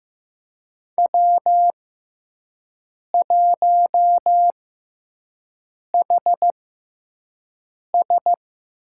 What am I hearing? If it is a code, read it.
Morse code, heard as a keyed tone, about 15 wpm, spaced as for 5 wpm, 700 Hz. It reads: W1HS
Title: USA Call Signs - 6x Character Spacing - 15wpm